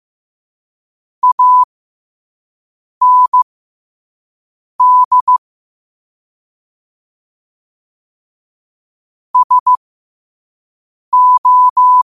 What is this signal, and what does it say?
Morse code, heard as a keyed tone, about 15 wpm, spaced as for 5 wpm, 1000 Hz. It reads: AND SO